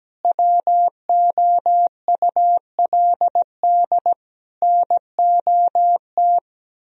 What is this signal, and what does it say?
Morse code, heard as a keyed tone, about 17 wpm, 700 Hz. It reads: WOULD NOT
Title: Sentences from Top 1000 Words - Review Part 4 - 17wpm